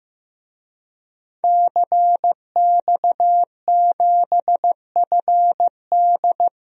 Morse code, heard as a keyed tone, about 15 wpm, 700 Hz. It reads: CX7FD
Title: DXCC-100 Call Signs - 5 Characters - Repeated 2x - 15wpm